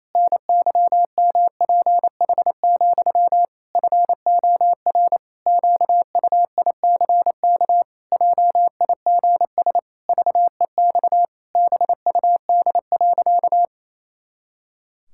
Morse code, heard as a keyed tone, about 28 wpm, 700 Hz. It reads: NYMPH, FOR QUICK JIGS VEX BUD.
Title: Pangram Challenge v2 - 28wpm